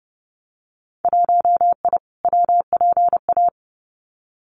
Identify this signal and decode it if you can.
Morse code, heard as a keyed tone, about 31 wpm, 700 Hz. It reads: 1I WPA
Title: Winter Field Day Exchanges Encore 30wpm